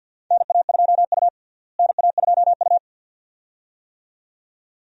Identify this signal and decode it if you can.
Morse code, heard as a keyed tone, about 53 wpm, 690 Hz. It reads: NA2U NA2U
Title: USA Call Signs - 4 Characters - Repeated 2x - 50wpm